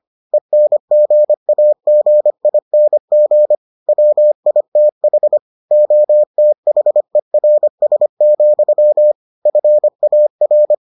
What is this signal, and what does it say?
Morse code, heard as a keyed tone, about 25 wpm, 600 Hz. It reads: ENGAGING WITH OTHERS, FAR